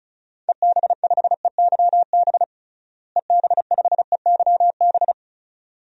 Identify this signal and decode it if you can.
Morse code, heard as a keyed tone, about 34 wpm, 700 Hz. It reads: EB5EYB EB5EYB
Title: DXCC-100 Call Signs - 6 Characters - Repeated 2x - 35wpm